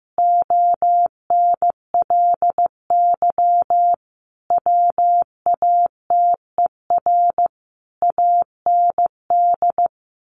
Morse code, heard as a keyed tone, about 15 wpm, 700 Hz. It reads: ONLY WATER AND
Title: Sentences from Top 1200 Words 15wpm